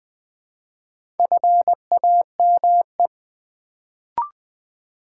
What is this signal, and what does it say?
Morse code, heard as a keyed tone, about 20 wpm, 700 Hz. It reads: FAME
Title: Build A Word - 1 Letter At A Time - 4-Letter Words - 20wpm